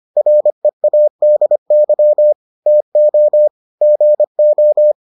Morse code, heard as a keyed tone, about 25 wpm, 600 Hz. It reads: READY TO GO